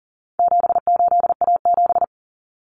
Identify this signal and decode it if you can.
Morse code, heard as a keyed tone, about 40 wpm, 700 Hz. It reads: 78A7
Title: Common Cut Numbers - 4-Digit 40wpm